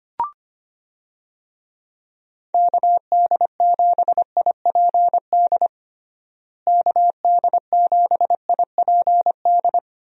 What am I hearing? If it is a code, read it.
Morse code, heard as a keyed tone, about 25 wpm, 700 Hz. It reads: KD7IPD KD7IPD
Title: USA Call Signs - 6 Characters - Repeated 2x - 25wpm